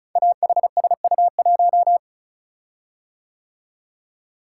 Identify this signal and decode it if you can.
Morse code, heard as a keyed tone, about 36 wpm, 700 Hz. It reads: AHSU1
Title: Code Groups - 5 Letter:Numbers - 35wpm